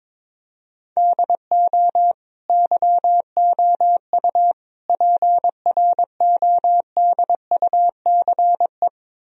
Morse code, heard as a keyed tone, about 21 wpm, 700 Hz. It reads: DO YOU PRODUCE
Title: Sentences from Top 1000 Words - Review Part 4 - 22wpm